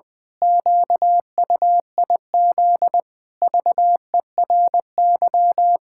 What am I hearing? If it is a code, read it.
Morse code, heard as a keyed tone, about 20 wpm, 700 Hz. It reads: QUIZ VERY
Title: Pangram Challenge v2 - 20wpm